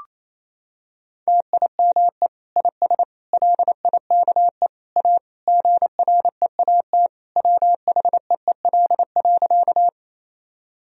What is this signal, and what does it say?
Morse code, heard as a keyed tone, about 28 wpm, 700 Hz. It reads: TIME IS LIKE A GREAT WHEEL.
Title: Sentences from Top 1000 Words - Review Part 3 - 28wpm